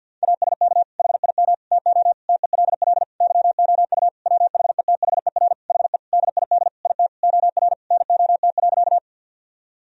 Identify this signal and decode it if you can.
Morse code, heard as a keyed tone, about 50 wpm, 700 Hz. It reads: ASK HIM TO TELL YOU WHETHER HE DID IT OR NOT.
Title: Sentences from Top 1000 Words - Review Part 3 - 50wpm